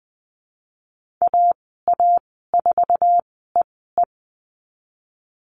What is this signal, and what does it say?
Morse code, heard as a keyed tone, about 20 wpm, spaced as for 15 wpm, 700 Hz. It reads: AA4EE
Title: 048 Lesson 9g - Number 4 - Call Signs 20wpm